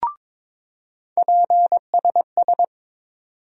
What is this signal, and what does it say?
Morse code, heard as a keyed tone, about 22 wpm, 700 Hz. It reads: PSS
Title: USA Call Sign Postfixes - 3 Letters 22wpm